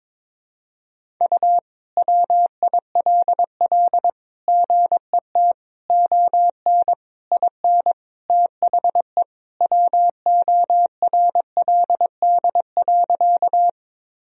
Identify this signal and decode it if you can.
Morse code, heard as a keyed tone, about 22 wpm, 700 Hz. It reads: U WILL GET ON IN THE WORLD.
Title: Sentences from Top 200 Words and Abbreviations 22wpm